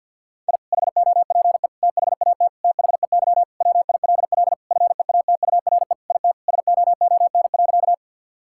Morse code, heard as a keyed tone, about 51 wpm, 700 Hz. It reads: I HOPE THAT THEY WILL FEATURE IT SOON.
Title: Sentences from Top 1000 Words - Review Part 5 - 50wpm